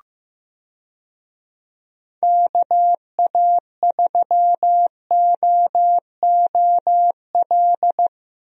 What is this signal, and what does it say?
Morse code, heard as a keyed tone, about 15 wpm, 700 Hz. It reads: KA3OOL
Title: USA Call Signs - 6 Characters - Repeated 3x - 15wpm